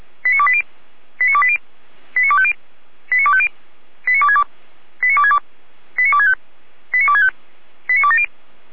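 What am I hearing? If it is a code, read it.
Five-tone selective call (ZVEI), heard as quick runs of five tones, tones of about 70 ms. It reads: 89280 89280 89250 89250 89262 89262 89265 89265 89270
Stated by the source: ZVEI five-tone selective call, code 89246, received with an SDR.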